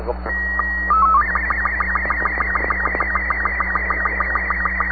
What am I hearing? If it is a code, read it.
SSTV (slow-scan television), Robot 36, with speech.
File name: sat / AO-51 14Jul2010 SSTV